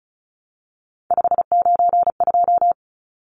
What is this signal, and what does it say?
Morse code, heard as a keyed tone, about 34 wpm, 700 Hz. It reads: 592
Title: Numbers - 3-Digit 35wpm